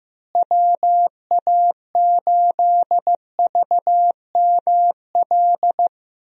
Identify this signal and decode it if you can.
Morse code, heard as a keyed tone, about 15 wpm, 700 Hz. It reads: WA8VML
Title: USA Call Signs - 6 Characters - Repeated 3x - 15wpm